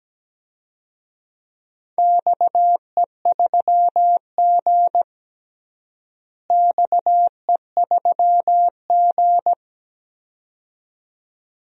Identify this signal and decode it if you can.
Morse code, heard as a keyed tone, about 17 wpm, 700 Hz. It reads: XE3G XE3G
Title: DXCC-100 Call Signs - 4 Characters - Repeated 2x - 17wpm